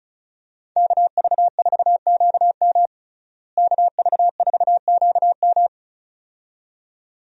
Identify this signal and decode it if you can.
Morse code, heard as a keyed tone, about 36 wpm, 700 Hz. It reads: KV4QM KV4QM
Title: USA Call Signs - 5 Characters - Repeated 2x - 35wpm